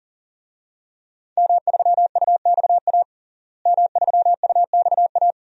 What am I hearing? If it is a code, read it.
Morse code, heard as a keyed tone, about 40 wpm, 700 Hz. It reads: M3UXA M3UXA
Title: DXCC-100 Call Signs - 5 Characters - Repeated 2x - 40wpm